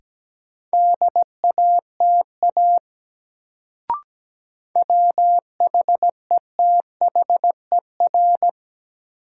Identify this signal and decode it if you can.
Morse code, heard as a keyed tone, about 17 wpm, 700 Hz. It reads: DATA WHETHER
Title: Sentences from Top 1000 Words - Review Part 3 - New Word Drill - 17wpm